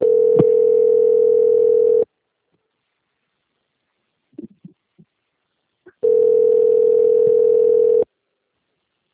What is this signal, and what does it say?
Ringback tone (North America), 440 + 480 Hz.
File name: out- - - - - .5096